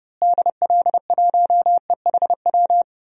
Morse code, heard as a keyed tone, about 30 wpm, 700 Hz. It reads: DL1EHW
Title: DXCC-100 Call Signs - 6 Characters - Repeated 2x - 30wpm